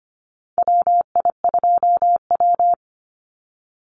Morse code, heard as a keyed tone, about 25 wpm, 700 Hz. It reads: WI2W
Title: NA SOTA Chaser Callsigns 25wpm